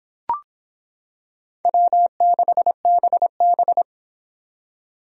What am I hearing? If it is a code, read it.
Morse code, heard as a keyed tone, about 25 wpm, 700 Hz. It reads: W6BB